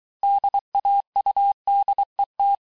Morse code, heard as a keyed tone, about 23 wpm, 800 Hz. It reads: DAUDET
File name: cours-du-mercredi / audios / COURS-855--texte-clair-vitesse-22-mpm